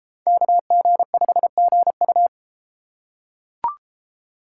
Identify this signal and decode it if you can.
Morse code, heard as a keyed tone, about 30 wpm, 700 Hz. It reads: KG5GU